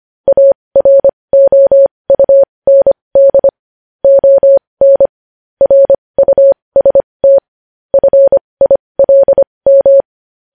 Morse code, heard as a keyed tone, about 25 wpm, spaced as for 20 wpm, 550 Hz. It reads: AROUND ON RUST FILM
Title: News Headlines at 20